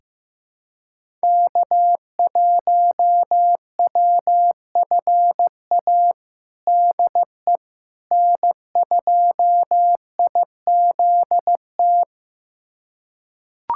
Morse code, heard as a keyed tone, about 15 wpm, 700 Hz. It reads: K1WFA DE N2IZT
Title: USA Callsign DE Callsign - 5 Characters - 15wpm